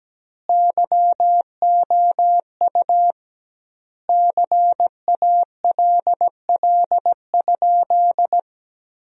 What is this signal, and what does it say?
Morse code, heard as a keyed tone, about 17 wpm, 700 Hz. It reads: YOU CALL?
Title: 2x Word-Spacing - Sentences from Top 100 Words - 17wpm